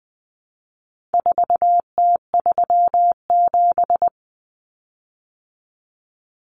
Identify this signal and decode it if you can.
Morse code, heard as a keyed tone, about 20 wpm, 700 Hz. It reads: 4T37
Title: Common Cut Numbers - 4-Digit 20wpm